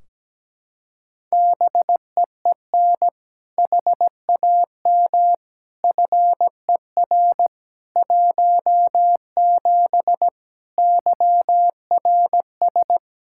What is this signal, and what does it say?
Morse code, heard as a keyed tone, about 17 wpm, 700 Hz. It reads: BEEN HAM FER 17 YRS
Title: Practice QSO - Exchange 3 - More Info - 17wpm